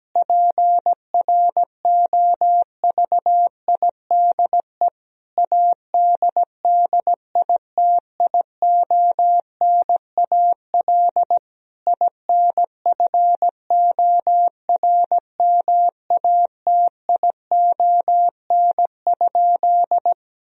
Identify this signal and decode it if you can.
Morse code, heard as a keyed tone, about 17 wpm, 700 Hz. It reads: PROVIDE ADDITIONAL INFORMATION?
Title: Sentences from Top 1000 Words - Review Part 4 - 17wpm